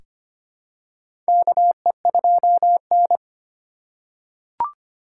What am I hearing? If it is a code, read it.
Morse code, heard as a keyed tone, about 25 wpm, 700 Hz. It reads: KE2N
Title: USA Call Signs - 4 Characters - 3x Character Spacing - 25wpm